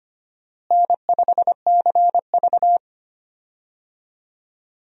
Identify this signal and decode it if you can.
Morse code, heard as a keyed tone, about 25 wpm, 700 Hz. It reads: N5CV
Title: USA Call Signs - 4 Characters - Repeated 3x - 25wpm